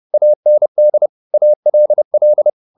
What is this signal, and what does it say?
Morse code, heard as a keyed tone, about 30 wpm, 600 Hz. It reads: AND ALL